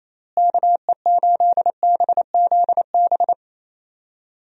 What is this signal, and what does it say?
Morse code, heard as a keyed tone, about 28 wpm, 700 Hz. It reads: KE8BZB
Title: USA Call Signs - 6 Characters - Repeated 3x - 28wpm